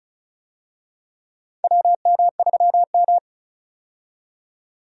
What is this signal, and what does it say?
Morse code, heard as a keyed tone, about 36 wpm, 700 Hz. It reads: WM3M
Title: USA Call Signs - 4 Characters - 35wpm